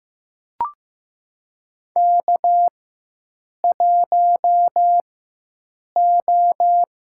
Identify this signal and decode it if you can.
Morse code, heard as a keyed tone, about 15 wpm, spaced as for 7 wpm, 700 Hz. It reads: K1O
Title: USA Call Signs - 4x Character Spacing - 15wpm